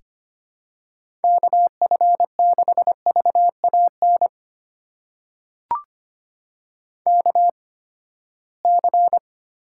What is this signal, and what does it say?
Morse code, heard as a keyed tone, about 25 wpm, spaced as for 17 wpm, 700 Hz. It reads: KF6VAN KC
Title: USA Call Signs - 6 Characters - 8x Character Spacing - 25wpm